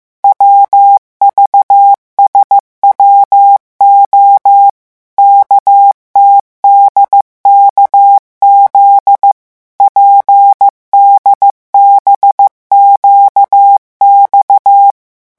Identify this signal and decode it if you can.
Morse code, heard as a keyed tone, about 15 wpm, 800 Hz. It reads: WVSWO KTDKZ PDBQX
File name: morse mp3 / revision11 lettres vitesse 14